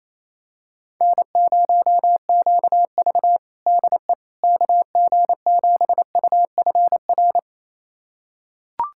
Morse code, heard as a keyed tone, about 28 wpm, 700 Hz. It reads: N0QV DE KG7UFR
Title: USA Callsign DE Callsign 28wpm